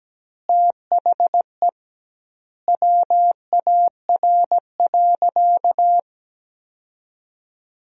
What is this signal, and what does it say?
Morse code, heard as a keyed tone, about 17 wpm, 700 Hz. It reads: THE WAR.
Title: 2x Word-Spacing - Sentences from Top 400 Words - 17wpm